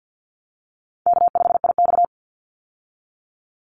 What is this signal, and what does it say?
Morse code, heard as a keyed tone, about 50 wpm, 700 Hz. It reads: K5IX
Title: Call Signs Repeated 3x - 50wpm